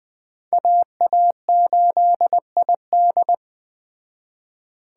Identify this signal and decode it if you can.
Morse code, heard as a keyed tone, about 20 wpm, 700 Hz. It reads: AA8ID